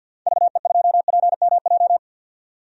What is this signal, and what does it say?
Morse code, heard as a keyed tone, about 47 wpm, 700 Hz. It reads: VE2PMJ
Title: DXCC-100 Call Signs - 6 Characters - 50wpm